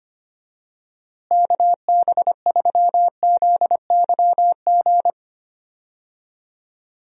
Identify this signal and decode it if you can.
Morse code, heard as a keyed tone, about 25 wpm, 700 Hz. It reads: KB3ZYG
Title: USA Call Signs - 6 Characters - Repeated 3x - 25wpm